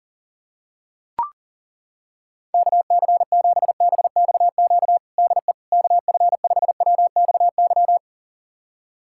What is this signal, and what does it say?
Morse code, heard as a keyed tone, about 40 wpm, 700 Hz. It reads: KC7BXQ DE KF5WXY
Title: USA Callsign DE Callsign 40wpm